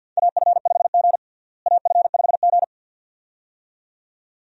Morse code, heard as a keyed tone, about 51 wpm, 700 Hz. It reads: AF5G AF5G
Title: USA Call Signs - 4 Characters - Repeated 2x - 50wpm